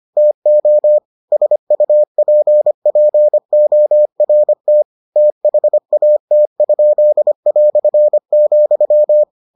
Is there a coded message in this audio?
Yes: Morse code from 0.2 to 9.2 s, heard as a keyed tone, about 25 wpm, 600 Hz. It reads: TO SUPPORT THAT?",